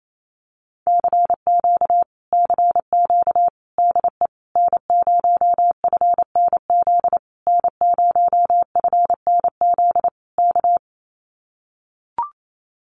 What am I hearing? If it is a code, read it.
Morse code, heard as a keyed tone, about 28 wpm, 700 Hz. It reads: CQ CQ DE N0FNZ N0FNZ K
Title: Common QSO Phrases Encore 28wpm